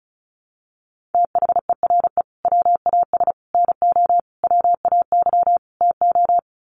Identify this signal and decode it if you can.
Morse code, heard as a keyed tone, about 34 wpm, 700 Hz. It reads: THERE WAS NO WAY TO
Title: Sets of 5 Words Encore 35wpm